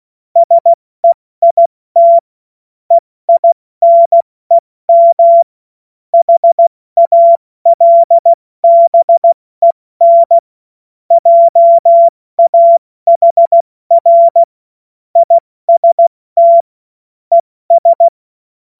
Morse code, heard as a keyed tone, about 16 wpm, 680 Hz. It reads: SEIT EINEM HALBEN JAHR IST ES
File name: Morsekurs / Witze / Witz 19. April 2022 70 F80